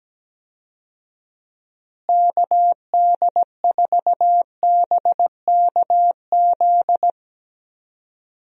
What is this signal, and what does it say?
Morse code, heard as a keyed tone, about 17 wpm, 700 Hz. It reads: KD4BKZ
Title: USA Call Signs - 6 Characters - Repeated 3x - 17wpm